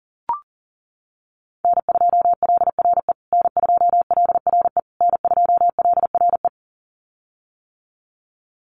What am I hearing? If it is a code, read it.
Morse code, heard as a keyed tone, about 40 wpm, 700 Hz. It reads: N2LRE N2LRE N2LRE
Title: USA Call Signs Repeated 3x 40wpm